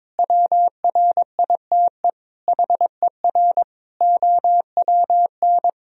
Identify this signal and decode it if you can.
Morse code, heard as a keyed tone, about 22 wpm, 700 Hz. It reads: WRITE HER OWN
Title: Sentences from Top 1000 Words - Review Part 1 - 22wpm